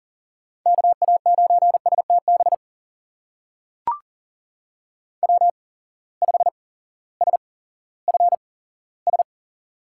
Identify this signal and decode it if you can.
Morse code, heard as a keyed tone, about 40 wpm, spaced as for 18 wpm, 700 Hz. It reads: KA9STB W5SFS
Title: USA Call Signs - 8x Character Spacing - 40wpm